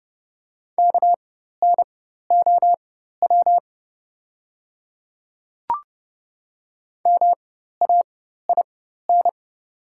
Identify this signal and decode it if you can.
Morse code, heard as a keyed tone, about 30 wpm, spaced as for 14 wpm, 700 Hz. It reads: KNOW MAIN
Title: 4x Character Spacing - Top 300 Words - 30wpm